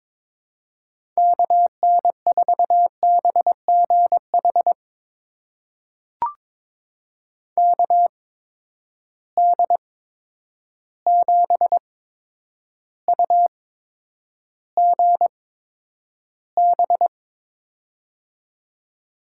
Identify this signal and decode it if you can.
Morse code, heard as a keyed tone, about 23 wpm, spaced as for 10 wpm, 700 Hz. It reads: KN4BGH KD7UGB
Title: USA Call Signs - 6 Characters - 8x Character Spacing - 22wpm